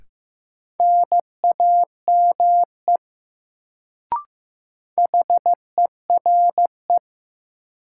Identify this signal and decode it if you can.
Morse code, heard as a keyed tone, about 15 wpm, 700 Hz. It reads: NAME HERE
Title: Sentences from Top 200 Words - New Word Drill - 15wpm